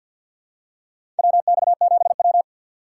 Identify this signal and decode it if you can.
Morse code, heard as a keyed tone, about 50 wpm, 700 Hz. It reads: WX7W
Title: USA Call Signs - 6x Character Spacing - 50wpm